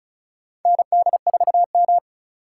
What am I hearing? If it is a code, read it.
Morse code, heard as a keyed tone, about 34 wpm, 700 Hz. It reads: ND4M
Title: USA Call Signs - 4 Characters - 3x Character Spacing - 35wpm